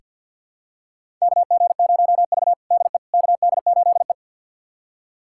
Morse code, heard as a keyed tone, about 49 wpm, 700 Hz. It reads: KG0V DE KD8E
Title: USA Callsign DE Callsign - 4 Characters - 50wpm